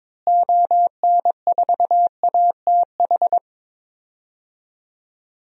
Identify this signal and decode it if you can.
Morse code, heard as a keyed tone, about 22 wpm, 700 Hz. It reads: ON4ATH
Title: DXCC-100 Call Signs - 6 Characters - Repeated 2x - 22wpm